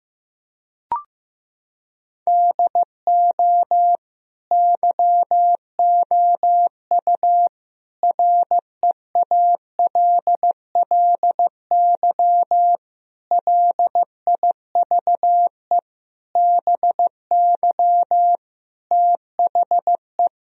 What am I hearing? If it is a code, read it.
Morse code, heard as a keyed tone, about 16 wpm, 700 Hz. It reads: DO YOU REALLY LIVE BY THE
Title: Sentences from Top 1000 Words - Review Part 1 - 15wpm